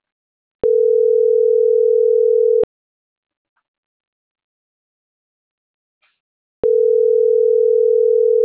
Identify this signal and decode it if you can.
Ringback tone (North America), 440 + 480 Hz.
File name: CENTRAL MABE / out mariel.reyes